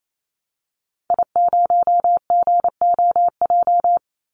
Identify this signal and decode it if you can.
Morse code, heard as a keyed tone, about 28 wpm, 700 Hz. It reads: I0GOJ
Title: Call Signs - 28wpm